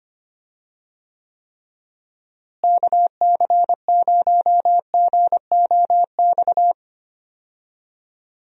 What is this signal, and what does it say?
Morse code, heard as a keyed tone, about 25 wpm, 700 Hz. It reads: KC0GOX
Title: USA Call Signs - 6 Characters - Repeated 3x - 25wpm